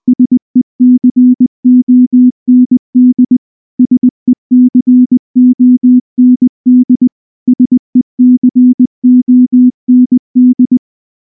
Morse code, heard as a keyed tone, about 19 wpm, 250 Hz. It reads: SECOND SECOND SECOND
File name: morse / morse-training-20WPM-6